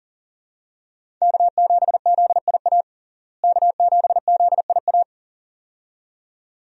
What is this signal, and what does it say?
Morse code, heard as a keyed tone, about 37 wpm, 700 Hz. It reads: K7ZIA K7ZIA
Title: USA Call Signs - 5 Characters - Repeated 2x - 40wpm